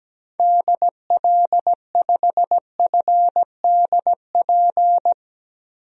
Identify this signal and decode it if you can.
Morse code, heard as a keyed tone, about 17 wpm, 700 Hz. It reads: DL5FDP
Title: DXCC-100 Call Signs - 6 Characters - 17wpm